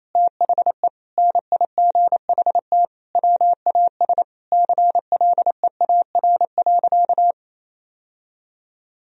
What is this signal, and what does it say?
Morse code, heard as a keyed tone, about 28 wpm, 700 Hz. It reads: THE NIGHT WAS CLEAR.
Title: Sentences from Top 1000 Words - Review Part 1 - 28wpm